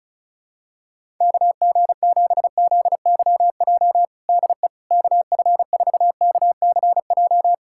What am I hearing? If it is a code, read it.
Morse code, heard as a keyed tone, about 36 wpm, 700 Hz. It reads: KG7ZYJ DE KF4KCJ
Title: USA Callsign DE Callsign - 6 Characters - 35wpm